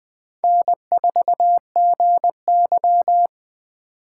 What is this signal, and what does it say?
Morse code, heard as a keyed tone, about 21 wpm, 700 Hz. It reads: N4GY
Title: Speed Racing - USA 1x2 Call Signs 17wpm